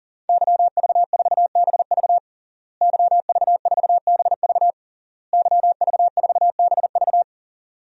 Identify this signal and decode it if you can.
Morse code, heard as a keyed tone, about 40 wpm, 700 Hz. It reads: YV4BV YV4BV YV4BV
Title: DXCC-100 Call Signs - 5 Characters - Repeated 3x - 40wpm